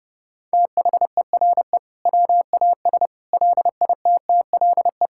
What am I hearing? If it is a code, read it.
Morse code, heard as a keyed tone, about 30 wpm, 700 Hz. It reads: THERE WAS LITTLE
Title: Sentences from Top 1000 Words - Review Part 5 - 30wpm